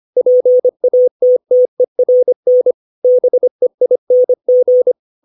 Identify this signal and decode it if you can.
Morse code, heard as a keyed tone, about 25 wpm, 500 Hz. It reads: PATTERN BEING
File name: 632 words / 632 Words 25WPM